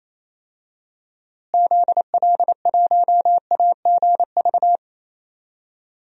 Morse code, heard as a keyed tone, about 28 wpm, 700 Hz. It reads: ZL1AGV
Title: DXCC-100 Call Signs - 6 Characters - Repeated 3x - 28wpm